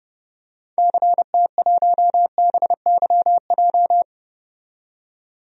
Morse code, heard as a keyed tone, about 30 wpm, 700 Hz. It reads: CT1BYJ
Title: DXCC-100 Call Signs - 6 Characters - 30wpm